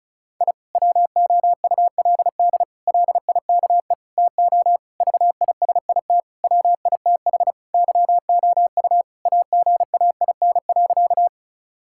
Morse code, heard as a keyed tone, about 36 wpm, 700 Hz. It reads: I WOULD LIKE TO VISIT WITH YOU AGAIN.
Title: Sentences from Top 1000 Words - Review Part 6 - 35wpm